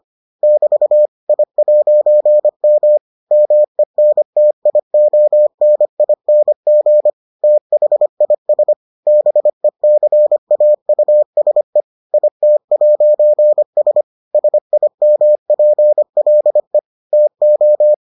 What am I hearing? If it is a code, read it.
Morse code, heard as a keyed tone, about 25 wpm, 600 Hz. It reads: <BT> I'M MENTIONING THIS BECAUSE IT'S SIMPLE TO